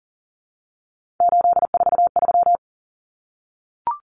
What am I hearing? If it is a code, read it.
Morse code, heard as a keyed tone, about 40 wpm, 700 Hz. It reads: 843
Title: Common Cut Numbers - 3-Digit 40wpm